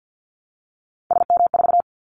Morse code, heard as a keyed tone, about 50 wpm, 700 Hz. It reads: SM4
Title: Callsign Prefixes for Top 25 DXCC - Repeated 3x - 50wpm